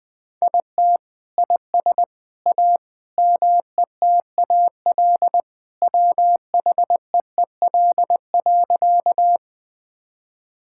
Morse code, heard as a keyed tone, about 20 wpm, 700 Hz. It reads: IT IS A METAL WHEEL.
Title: Sentences from Top 1000 Words - Review Part 6 - 20wpm